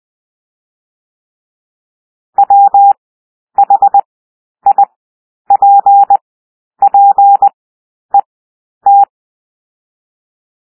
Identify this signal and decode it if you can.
Morse code, heard as a keyed tone, about 20 wpm, 800 Hz. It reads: W H I P P E T
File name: cwmp3 / W1 SOUND WORDS 20 wpm